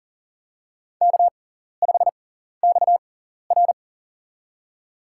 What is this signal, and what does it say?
Morse code, heard as a keyed tone, about 40 wpm, spaced as for 14 wpm, 700 Hz. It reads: K5XR
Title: USA Call Signs - 6x Character Spacing - 40wpm